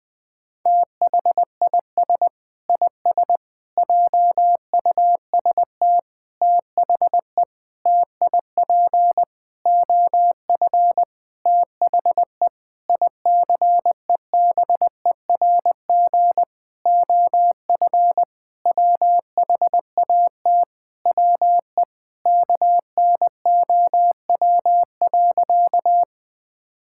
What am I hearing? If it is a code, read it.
Morse code, heard as a keyed tone, about 20 wpm, 700 Hz. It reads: THIS IS JUST THE TIP OF THE ICEBERG OF WHAT WE KNOW.
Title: Sentences with 8-Word Phrases - 20wpm